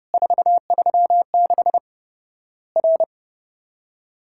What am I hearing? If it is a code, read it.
Morse code, heard as a keyed tone, about 30 wpm, 700 Hz. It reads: 436 R
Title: Worked All Europe DX Contest - QTC Exchange - 30wpm